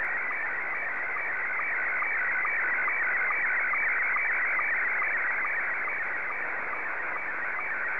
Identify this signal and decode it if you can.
SSTV (slow-scan television), Scottie 1.